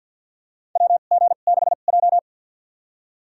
Morse code, heard as a keyed tone, about 51 wpm, spaced as for 36 wpm, 700 Hz. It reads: WG6J
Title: USA Call Signs - 4 Characters - 2x Character Spacing - 50wpm